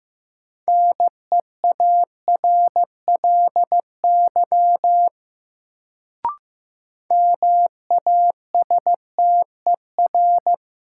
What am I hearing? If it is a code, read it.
Morse code, heard as a keyed tone, about 15 wpm, 700 Hz. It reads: NEARLY MASTER
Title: Sentences from Top 1300 Words - New Word Drill - 15wpm